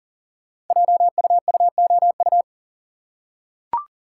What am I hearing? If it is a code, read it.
Morse code, heard as a keyed tone, about 40 wpm, 700 Hz. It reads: JUUOU
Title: Code Groups - 5 Letters - 40wpm